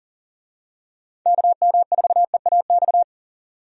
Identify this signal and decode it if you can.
Morse code, heard as a keyed tone, about 40 wpm, 700 Hz. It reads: KM4EAX
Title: USA Call Signs - 6 Characters - 40wpm